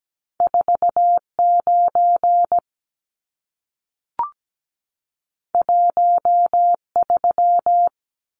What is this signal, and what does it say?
Morse code, heard as a keyed tone, about 17 wpm, 700 Hz. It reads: 49 13
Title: Numbers - 2-Digit 17wpm